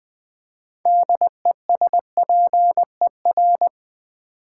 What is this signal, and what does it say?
Morse code, heard as a keyed tone, about 20 wpm, 700 Hz. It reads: DESPER
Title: Build A Word - 3 Letters At A Time - 9-Letter Words - 20wpm